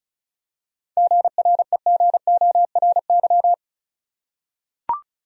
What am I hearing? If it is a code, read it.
Morse code, heard as a keyed tone, about 36 wpm, 700 Hz. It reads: GREGORY
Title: USA Common Names - 35wpm